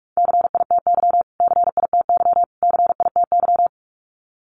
Morse code, heard as a keyed tone, about 44 wpm, 700 Hz. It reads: CITY CITY CITY
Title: Top 300 Words Repeated 45wpm